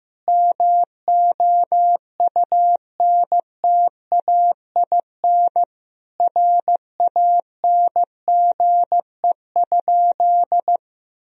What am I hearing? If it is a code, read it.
Morse code, heard as a keyed tone, about 15 wpm, 700 Hz. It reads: MOUNTAIN RANGE?
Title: Sentences from Top 1000 Words - Review Part 4 - 15wpm